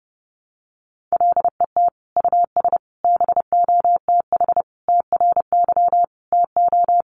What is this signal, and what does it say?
Morse code, heard as a keyed tone, about 28 wpm, 700 Hz. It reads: LET US BOTH TRY TO
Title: Sentences from Top 1200 Words 30wpm